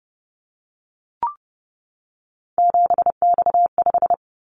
Morse code, heard as a keyed tone, about 30 wpm, 700 Hz. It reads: 7X5
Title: Callsign Prefixes for Top 100 DXCC - 30wpm